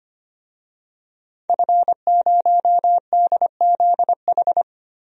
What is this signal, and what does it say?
Morse code, heard as a keyed tone, about 25 wpm, 700 Hz. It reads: F0DZH
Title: DXCC-100 Call Signs - 5 Characters - Repeated 2x - 25wpm